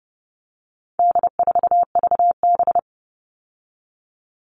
Morse code, heard as a keyed tone, about 30 wpm, 700 Hz. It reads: D4VB
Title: All Cut Numbers - 4-Digit 30wpm